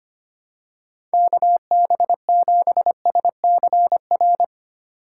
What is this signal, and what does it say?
Morse code, heard as a keyed tone, about 25 wpm, 700 Hz. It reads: KB7SCR
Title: USA Call Signs - 6 Characters - 2x Character Spacing - 25wpm